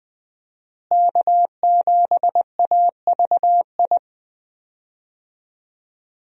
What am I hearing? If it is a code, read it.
Morse code, heard as a keyed tone, about 20 wpm, 700 Hz. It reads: K7AVI
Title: USA Call Signs - 5 Characters - Repeated 2x - 20wpm